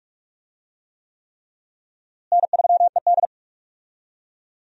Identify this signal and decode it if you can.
Morse code, heard as a keyed tone, about 45 wpm, 700 Hz. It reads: N3ED
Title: USA Call Signs - 4 Characters - Repeated 3x - 45wpm